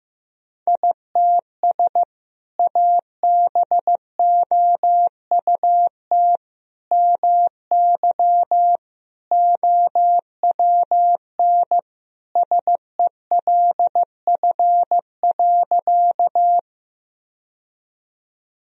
Morse code, heard as a keyed tone, about 16 wpm, 700 Hz. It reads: ITS ABOUT MY OWN SELF.
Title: Sentences from Top 1000 Words - Review Part 1 - 15wpm